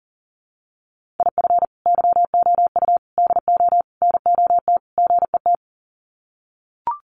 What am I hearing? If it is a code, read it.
Morse code, heard as a keyed tone, about 40 wpm, 700 Hz. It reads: IF YOU DO NOT GET
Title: Sets of 5 Words Encore2 40wpm